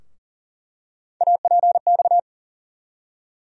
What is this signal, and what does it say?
Morse code, heard as a keyed tone, about 40 wpm, 700 Hz. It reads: APX
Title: USA Call Sign Postfixes - 3 Letters 40wpm